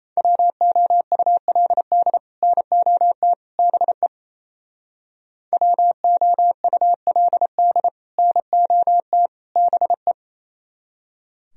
Morse code, heard as a keyed tone, about 36 wpm, 700 Hz. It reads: WOULD NOT BE WOULD NOT BE